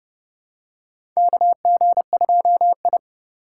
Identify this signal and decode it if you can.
Morse code, heard as a keyed tone, about 30 wpm, 700 Hz. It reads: KG2I
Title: USA Call Signs - 4 Characters - 4x Character Spacing - 30wpm